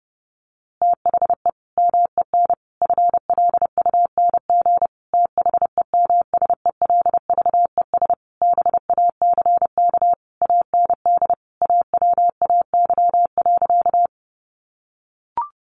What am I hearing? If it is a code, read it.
Morse code, heard as a keyed tone, about 28 wpm, 700 Hz. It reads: THE MEN FLUNG THEMSELVES BACK AND AWAY.
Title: A Sound Of Thunder 30wpm